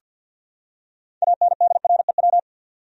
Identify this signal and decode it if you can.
Morse code, heard as a keyed tone, about 46 wpm, 700 Hz. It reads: ANDREW
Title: USA Common Names - 50wpm